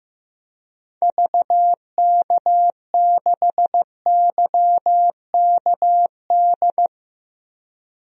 Morse code, heard as a keyed tone, about 15 wpm, 700 Hz. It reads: VK6YKD
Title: DXCC-100 Call Signs - 6 Characters - Repeated 2x - 15wpm